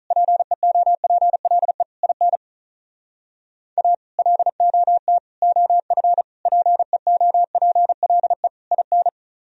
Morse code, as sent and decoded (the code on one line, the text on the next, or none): .--. --- .--. .-.. / .. -. / .- / .-.. --- - / --- ..-. / .--. --- .--. .-.. / .. -.
PEOPLE IN A LOT OF PEOPLE IN